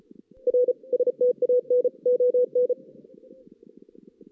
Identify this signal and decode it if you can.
Morse code, heard as a keyed tone, about 36 wpm, 500 Hz. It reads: R STAN ON
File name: 2016 JAN CW NAQP